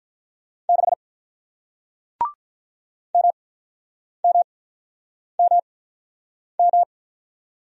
Morse code, heard as a keyed tone, about 48 wpm, spaced as for 9 wpm, 700 Hz. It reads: 6 MMMM